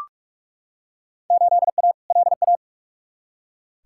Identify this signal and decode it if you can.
Morse code, heard as a keyed tone, about 46 wpm, 700 Hz. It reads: 8A LA
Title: Field Day Exchanges Encore - 45wpm